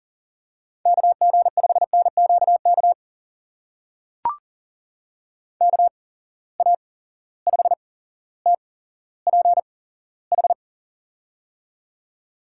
Morse code, heard as a keyed tone, about 39 wpm, spaced as for 17 wpm, 700 Hz. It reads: KG5NQK KA5TPH
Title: USA Call Signs - 6 Characters - 8x Character Spacing - 40wpm